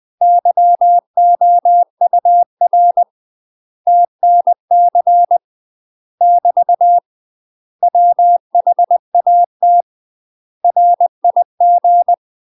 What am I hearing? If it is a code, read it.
Morse code, heard as a keyed tone, about 20 wpm, 700 Hz. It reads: YOUR TNC <BT> WHAT RIG
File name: UK MASTER QSOs TRAINING / 20WPM / UK MASTER QSOs 0056